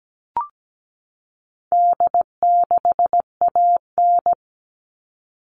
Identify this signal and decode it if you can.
Morse code, heard as a keyed tone, about 17 wpm, 700 Hz. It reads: D6AN
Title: All Cut Numbers - 4-Digit 17wpm